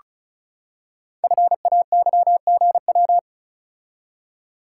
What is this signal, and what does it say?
Morse code, heard as a keyed tone, about 36 wpm, 690 Hz. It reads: FAYGW
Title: Code Groups - 5 Letters - 35wpm